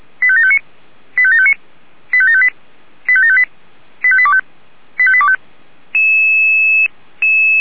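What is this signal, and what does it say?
Five-tone selective call (ZVEI), heard as quick runs of five tones, tones of about 70 ms. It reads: 85759 85759 85758 85758 85825 85825
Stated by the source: Five-tone ZVEI selective call received by SDR.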